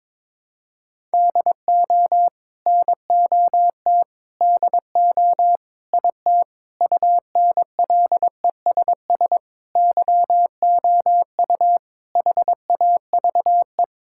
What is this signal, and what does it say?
Morse code, heard as a keyed tone, about 22 wpm, 700 Hz. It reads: DO NOT DO IT UNLESS YOU HAVE
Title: Sentences from Top 1000 Words - Review Part 6 - 22wpm